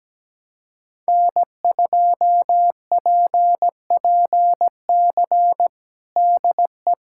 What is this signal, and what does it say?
Morse code, heard as a keyed tone, about 17 wpm, 700 Hz. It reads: N2PPC DE
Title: USA Callsign DE Callsign - 5 Characters - 17wpm